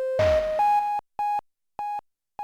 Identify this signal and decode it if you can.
There is no code.